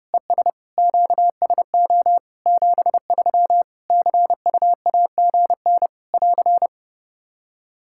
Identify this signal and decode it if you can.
Morse code, heard as a keyed tone, about 32 wpm, 700 Hz. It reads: ES QSO 73 CUAGN <AR>
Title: K1USN SST QSO Practice 30wpm